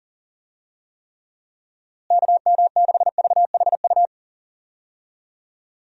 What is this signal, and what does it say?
Morse code, heard as a keyed tone, about 40 wpm, 700 Hz. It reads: KM6VHU
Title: USA Call Signs - 6 Characters - Repeated 3x - 40wpm